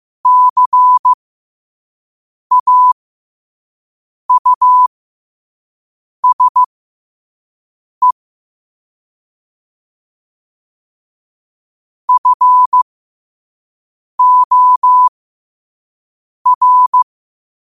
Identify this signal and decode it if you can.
Morse code, heard as a keyed tone, about 15 wpm, spaced as for 5 wpm, 1000 Hz. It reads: CAUSE FOR
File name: qotdfiles / 5wpm